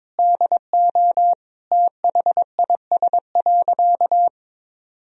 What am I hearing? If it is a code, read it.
Morse code, heard as a keyed tone, about 22 wpm, 700 Hz. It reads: DO THIS.
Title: Sentences from Top 1000 Words - Review Part 2 - 22wpm